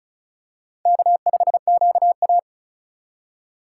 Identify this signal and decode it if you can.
Morse code, heard as a keyed tone, about 36 wpm, 690 Hz. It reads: K5QA
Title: USA Call Signs - 4 Characters - 2x Character Spacing - 35wpm